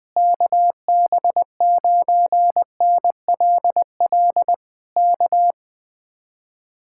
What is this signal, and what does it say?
Morse code, heard as a keyed tone, about 20 wpm, 700 Hz. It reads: KB9NLL K
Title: Speed-Racing Common QSO Phrases - 17wpm